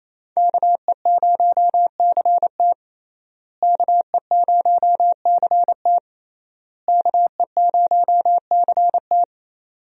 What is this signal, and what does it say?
Morse code, heard as a keyed tone, about 27 wpm, 700 Hz. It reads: KE0CT KE0CT KE0CT
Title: USA Call Signs - 5 Characters - Repeated 3x - 28wpm